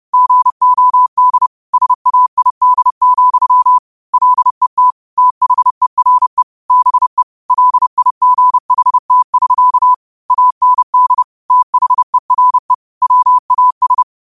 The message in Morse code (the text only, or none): GOD SAID, LET THERE BE LIGHT<SK> AND THERE WAS